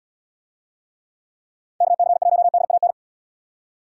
Morse code, heard as a keyed tone, about 70 wpm, 700 Hz. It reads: KC1DAN